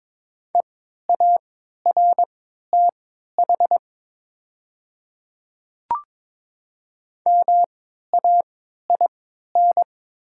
Morse code, heard as a keyed tone, about 22 wpm, spaced as for 12 wpm, 700 Hz. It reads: EARTH MAIN
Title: 3x Character Spacing - Top 300 Words - 22wpm